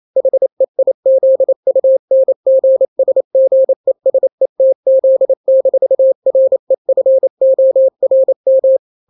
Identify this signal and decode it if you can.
Morse code, heard as a keyed tone, about 28 wpm, 530 Hz. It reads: HEIZUNGSGESETZ-REFORM
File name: daily / 27wpm